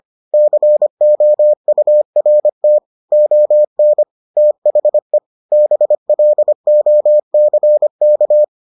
Morse code, heard as a keyed tone, about 25 wpm, 600 Hz. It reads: COURT ON THE BLOCK